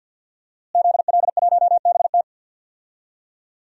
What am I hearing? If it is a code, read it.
Morse code, heard as a keyed tone, about 46 wpm, 700 Hz. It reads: ZL1BT
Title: DXCC-100 Call Signs - 5 Characters - Repeated 2x - 50wpm